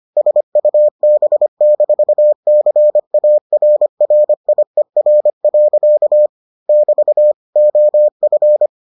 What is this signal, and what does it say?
Morse code, heard as a keyed tone, about 25 wpm, 600 Hz. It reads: SUB-CARRIER. <BT> OF